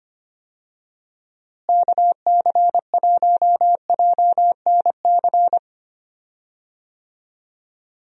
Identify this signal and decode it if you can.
Morse code, heard as a keyed tone, about 25 wpm, 700 Hz. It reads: KC1JNC
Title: USA Call Signs - 6 Characters - Repeated 2x - 25wpm